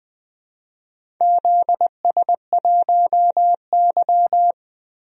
Morse code, heard as a keyed tone, about 20 wpm, 700 Hz. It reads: ZS1Y
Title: DXCC-100 Call Signs - 4 Characters - Repeated 2x - 20wpm